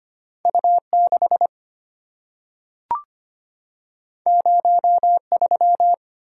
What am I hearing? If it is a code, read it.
Morse code, heard as a keyed tone, about 25 wpm, 700 Hz. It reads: U6 03
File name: Two Letter-Number Combinations - 25wpm